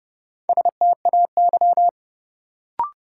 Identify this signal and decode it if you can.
Morse code, heard as a keyed tone, about 30 wpm, 700 Hz. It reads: STAY
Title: Sentences from Top 500 Words - New Word Drill - 30wpm